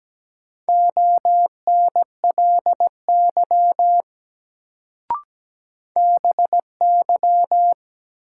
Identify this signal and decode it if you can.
Morse code, heard as a keyed tone, about 17 wpm, 700 Hz. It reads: ONLY BY
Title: Sets of 2 Words - New Word Drill - 17wpm